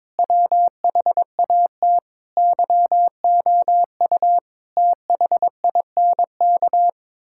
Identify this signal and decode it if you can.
Morse code, heard as a keyed tone, about 22 wpm, 700 Hz. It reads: WHAT YOU THINK
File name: practice / 22wpm / Sets of 7 Words - 22wpm